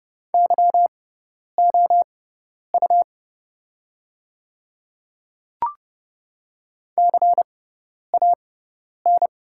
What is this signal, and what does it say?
Morse code, heard as a keyed tone, about 30 wpm, spaced as for 10 wpm, 690 Hz. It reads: YOU CAN
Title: 6x Character Spacing - Top 100 Words - 30wpm